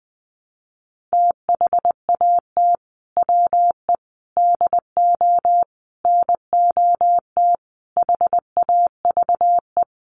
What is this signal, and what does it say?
Morse code, heard as a keyed tone, about 20 wpm, 700 Hz. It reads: THAT WE DO NOT HAVE
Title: Sets of 5 Words Encore 20wpm